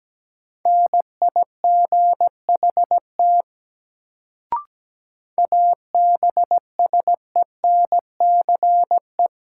Morse code, heard as a keyed tone, about 17 wpm, 700 Hz. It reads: NIGHT ABSENCE
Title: Sets of 4 Words - New Word Drill - 17wpm